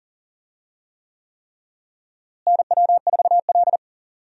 Morse code, heard as a keyed tone, about 40 wpm, 700 Hz. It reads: NW4L
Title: USA Call Signs - 4 Characters - Repeated 3x - 40wpm